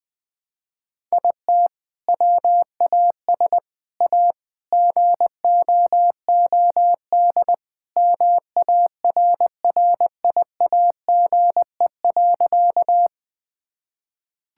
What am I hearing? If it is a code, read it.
Morse code, heard as a keyed tone, about 20 wpm, 700 Hz. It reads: IT WAS A GOOD MARRIAGE.
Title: Sentences from Top 1000 Words - Review Part 5 - 20wpm